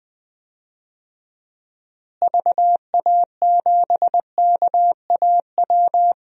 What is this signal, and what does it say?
Morse code, heard as a keyed tone, about 20 wpm, 700 Hz. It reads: VA7KAW
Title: DXCC-100 Call Signs - 6 Characters - Repeated 2x - 20wpm